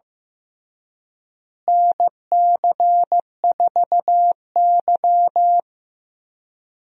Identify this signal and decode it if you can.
Morse code, heard as a keyed tone, about 15 wpm, 700 Hz. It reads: NC4Y
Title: USA Call Signs - 4 Characters - Repeated 2x - 15wpm